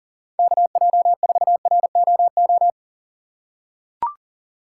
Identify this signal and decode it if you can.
Morse code, heard as a keyed tone, about 38 wpm, 700 Hz. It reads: KJ4ROO
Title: USA Call Signs - 6 Characters - 4x Character Spacing - 40wpm